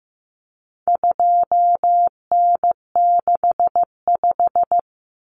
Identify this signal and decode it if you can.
Morse code, heard as a keyed tone, about 15 wpm, 700 Hz. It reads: 2N65
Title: Common Cut Numbers - 4-Digit 15wpm